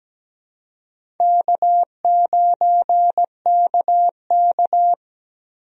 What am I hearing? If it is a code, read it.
Morse code, heard as a keyed tone, about 17 wpm, 700 Hz. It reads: K9KK
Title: USA Call Signs - 4 Characters - Repeated 3x - 17wpm